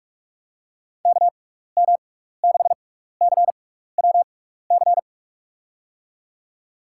Morse code, heard as a keyed tone, about 46 wpm, 700 Hz. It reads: K M 6 C W C
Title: USA Call Signs - 6 Characters - 6x Character Spacing - 45wpm